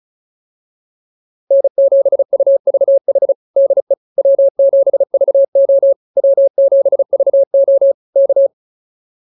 Morse code, heard as a keyed tone, about 34 wpm, 560 Hz. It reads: N7UVH DE W7VO W7VO K
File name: QSO105 35